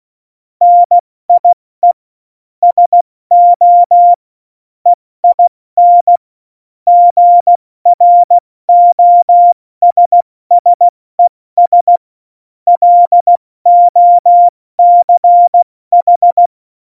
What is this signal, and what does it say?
Morse code, heard as a keyed tone, about 16 wpm, spaced as for 14 wpm, 700 Hz. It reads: NIE SO EIN GROSSES LOCH
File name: Morsekurs / Witze / Witz 05. Dezember 2017, 70 F80